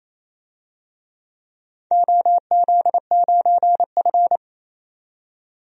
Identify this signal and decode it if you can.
Morse code, heard as a keyed tone, about 28 wpm, 700 Hz. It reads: OZ9F
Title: DXCC-100 Call Signs - 4 Characters - 28wpm